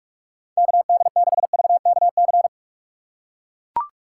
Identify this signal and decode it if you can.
Morse code, heard as a keyed tone, about 40 wpm, 700 Hz. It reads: KD6VKC